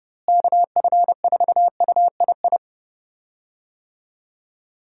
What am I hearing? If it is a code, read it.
Morse code, heard as a keyed tone, about 30 wpm, 700 Hz. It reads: KF4UII
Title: USA Call Signs - 6 Characters - Repeated 2x - 30wpm